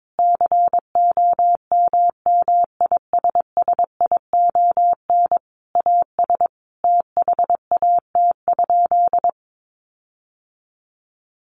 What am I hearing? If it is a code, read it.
Morse code, heard as a keyed tone, about 22 wpm, 700 Hz. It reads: COMMISSION AS THAT?
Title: Sentences from Top 1400 Words 22wpm